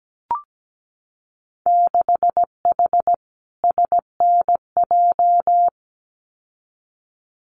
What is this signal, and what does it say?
Morse code, heard as a keyed tone, about 17 wpm, 700 Hz. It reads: 6H SNJ
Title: Winter Field Day Exchanges Encore 17wpm